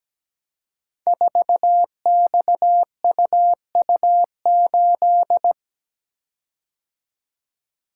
Morse code, heard as a keyed tone, about 17 wpm, 700 Hz. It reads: 4XUU8
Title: Code Groups - 5 Letter:Numbers - 17wpm